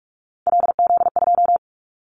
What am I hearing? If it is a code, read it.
Morse code, heard as a keyed tone, about 46 wpm, 700 Hz. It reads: LZ2
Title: Callsign Prefixes for Top 100 DXCC - 45wpm